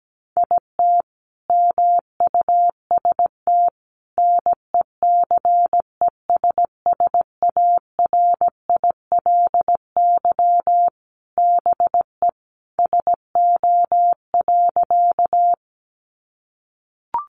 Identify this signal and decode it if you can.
Morse code, heard as a keyed tone, about 17 wpm, 700 Hz. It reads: IT MUST NECESSARILY BE SO.
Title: Sentences from Top 1400 Words 17wpm